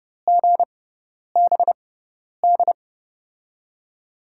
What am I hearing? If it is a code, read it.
Morse code, heard as a keyed tone, about 30 wpm, spaced as for 10 wpm, 700 Hz. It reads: GBD
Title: USA Call Signs - 6 Characters - 6x Character Spacing - 30wpm